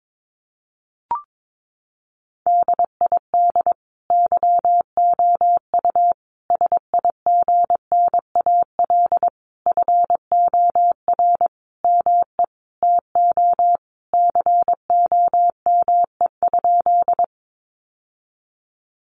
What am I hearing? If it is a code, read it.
Morse code, heard as a keyed tone, about 22 wpm, 700 Hz. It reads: DID YOU SIGNAL FOR ME TO COME?
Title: Sentences from Top 1400 Words 22wpm